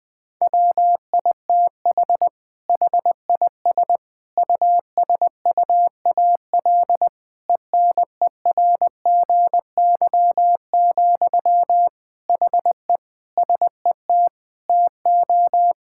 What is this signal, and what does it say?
Morse code, heard as a keyed tone, about 20 wpm, 700 Hz. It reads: WITH HIS USUAL ENERGY, HE SET TO
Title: Sentences from Top 1000 Words - Review Part 6 - 20wpm